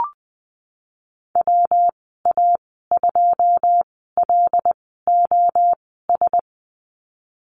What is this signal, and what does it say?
Morse code, heard as a keyed tone, about 20 wpm, spaced as for 15 wpm, 700 Hz. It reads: WA2LOS
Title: 193 Lesson 31g - Letter v - Call Signs 20wpm